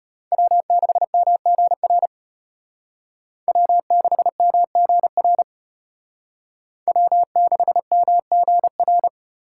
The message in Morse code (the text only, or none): W6MGR W6MGR W6MGR